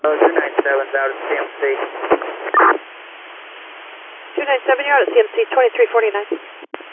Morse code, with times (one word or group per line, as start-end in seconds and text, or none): none